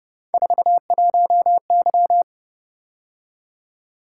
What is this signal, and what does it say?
Morse code, heard as a keyed tone, about 29 wpm, 700 Hz. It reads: 41Y
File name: Speed-Racing - Three Letter-Number Combinations - 25wpm